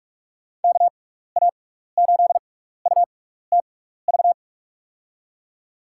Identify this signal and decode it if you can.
Morse code, heard as a keyed tone, about 46 wpm, spaced as for 16 wpm, 700 Hz. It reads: KA8UTV
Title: USA Call Signs - 6 Characters - 6x Character Spacing - 45wpm